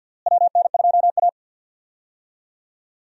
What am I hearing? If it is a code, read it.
Morse code, heard as a keyed tone, about 51 wpm, 700 Hz. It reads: WN2A
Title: USA Call Signs - 4 Characters - Repeated 2x - 50wpm